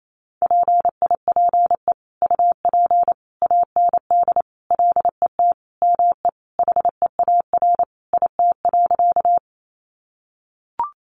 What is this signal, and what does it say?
Morse code, heard as a keyed tone, about 28 wpm, 700 Hz. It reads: PIPE UP AND LET ME HEAR IT.
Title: Sentences from Top 1500 Words 28wpm